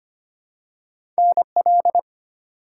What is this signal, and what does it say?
Morse code, heard as a keyed tone, about 25 wpm, 700 Hz. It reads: NL
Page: morse-code-ninja-practice-25wpm